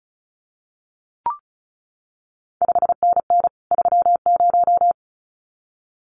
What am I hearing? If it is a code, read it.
Morse code, heard as a keyed tone, about 36 wpm, 700 Hz. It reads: 5NN 30
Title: CQ WW DX Contest Exchange Encore - 35wpm